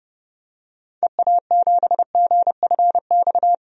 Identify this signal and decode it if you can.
Morse code, heard as a keyed tone, about 30 wpm, 700 Hz. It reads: EA7GFX
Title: DXCC-100 Call Signs - 6 Characters - 30wpm